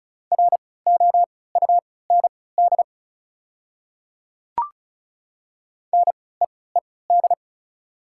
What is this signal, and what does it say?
Morse code, heard as a keyed tone, about 34 wpm, spaced as for 20 wpm, 690 Hz. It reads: ROUND NEED
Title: 3x Character Spacing - Top 500 Words - 35wpm